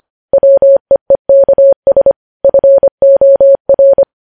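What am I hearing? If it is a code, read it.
Morse code, heard as a keyed tone, about 25 wpm, 550 Hz. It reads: WEEKS FOR
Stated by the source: Morse code news headlines at 25 wpm.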